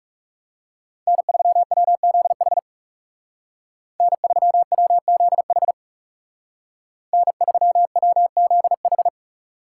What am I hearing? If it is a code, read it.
Morse code, heard as a keyed tone, about 39 wpm, 700 Hz. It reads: N3WZH N3WZH N3WZH